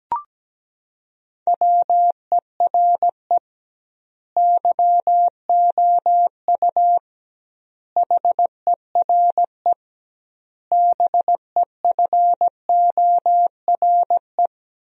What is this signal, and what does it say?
Morse code, heard as a keyed tone, about 17 wpm, 700 Hz. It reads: WERE YOU HERE BEFORE
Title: 2x Word-Spacing - Sentences from Top 200 Words - 17wpm